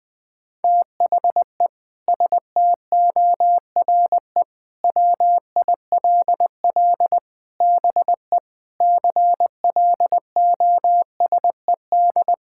Morse code, heard as a keyed tone, about 21 wpm, 700 Hz. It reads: THE STORE WILL BE CLOSED
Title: Sentences with 6-Word Phrases - 20wpm